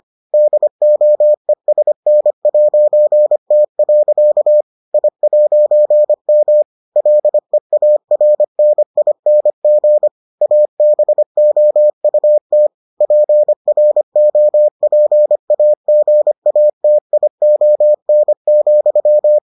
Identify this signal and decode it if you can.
Morse code, heard as a keyed tone, about 25 wpm, 600 Hz. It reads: DOESN'T. I'M LEARNING ABOUT PROPAGATION,